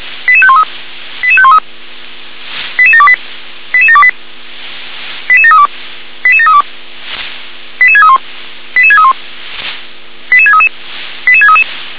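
Five-tone selective call (ZVEI), heard as quick runs of five tones, tones of about 70 ms. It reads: 88513 88513 88728 88728 88732 88732 88631 88631 88633 88633
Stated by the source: ZVEI five-tone selective call, received by SDR.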